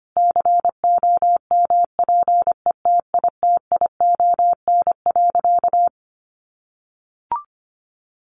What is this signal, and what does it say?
Morse code, heard as a keyed tone, about 25 wpm, 700 Hz. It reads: COMPETITION.
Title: Sentences from Top 1000 Words 25wpm